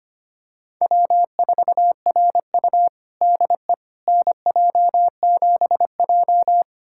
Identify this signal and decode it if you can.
Morse code, heard as a keyed tone, about 25 wpm, 700 Hz. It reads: W4RU DE NJ7J
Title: USA Callsign DE Callsign - 4 Characters - 25wpm